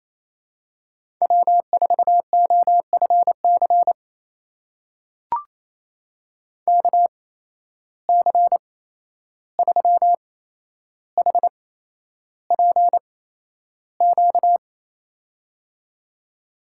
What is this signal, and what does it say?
Morse code, heard as a keyed tone, about 29 wpm, spaced as for 11 wpm, 700 Hz. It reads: W4OFC KC3HPQ
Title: USA Call Signs - 8x Character Spacing - 28wpm